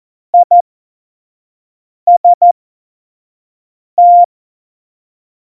Morse code, heard as a keyed tone, about 14 wpm, spaced as for 5 wpm, 700 Hz. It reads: IST
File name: Tempo 25 F70